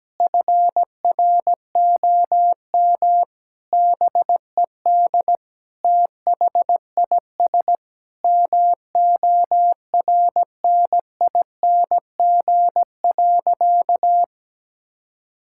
Morse code, heard as a keyed tone, about 17 wpm, 700 Hz. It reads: FROM BED THIS MORNING.
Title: Sentences from Top 1000 Words - Review Part 6 - 17wpm